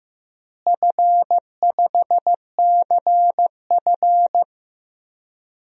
Morse code, heard as a keyed tone, about 15 wpm, 700 Hz. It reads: F5CF
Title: DXCC-100 Call Signs - 4 Characters - Repeated 3x - 15wpm